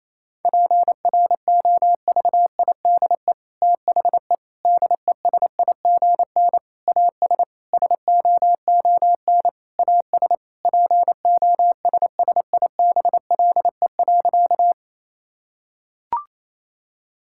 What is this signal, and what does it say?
Morse code, heard as a keyed tone, about 28 wpm, 700 Hz. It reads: PROVIDE THE DESIGN AS SOON AS POSSIBLE.
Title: Sentences from Top 1000 Words - Review Part 5 - 28wpm